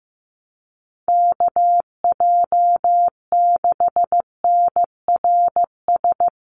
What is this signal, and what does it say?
Morse code, heard as a keyed tone, about 15 wpm, 700 Hz. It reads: KJ6NRS
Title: USA Call Signs 15wpm